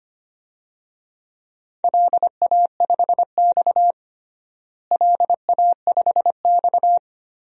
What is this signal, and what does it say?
Morse code, heard as a keyed tone, about 25 wpm, 700 Hz. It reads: LA5X LA5X
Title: DXCC-100 Call Signs - 4 Characters - Repeated 3x - 25wpm